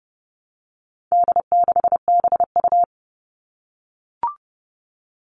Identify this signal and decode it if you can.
Morse code, heard as a keyed tone, about 30 wpm, 700 Hz. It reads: D6BU
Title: All Cut Numbers - 4-Digit 30wpm